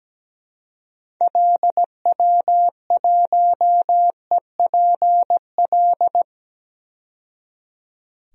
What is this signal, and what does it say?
Morse code, heard as a keyed tone, about 17 wpm, 700 Hz. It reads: LW1EPL
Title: DXCC-100 Call Signs - 6 Characters - 17wpm